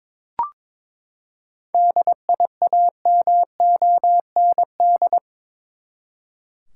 Morse code, heard as a keyed tone, about 22 wpm, 700 Hz. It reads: DIAMOND
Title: Seven Letter Words - 22wpm